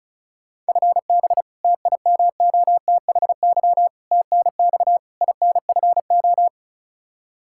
Morse code, heard as a keyed tone, about 36 wpm, 700 Hz. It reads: FB TIMOTHY TNX INFO
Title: Practice QSO - Exchange 3 - More Info - 35wpm